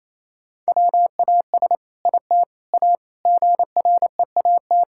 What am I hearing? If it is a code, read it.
Morse code, heard as a keyed tone, about 28 wpm, 700 Hz. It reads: WAS IT A GREAT
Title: Sentences from Top 1000 Words - Review Part 5 - 28wpm